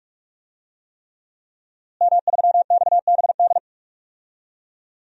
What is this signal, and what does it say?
Morse code, heard as a keyed tone, about 41 wpm, 700 Hz. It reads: M3XBD
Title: DXCC-100 Call Signs - 5 Characters - Repeated 3x - 45wpm